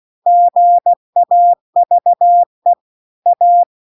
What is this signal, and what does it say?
Morse code, heard as a keyed tone, about 16 wpm, 700 Hz. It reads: GAVE A